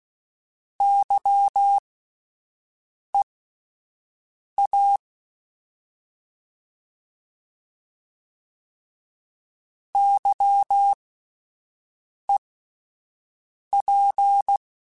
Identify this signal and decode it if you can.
Morse code, heard as a keyed tone, about 16 wpm, spaced as for 5 wpm, 800 Hz. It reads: YEA YEP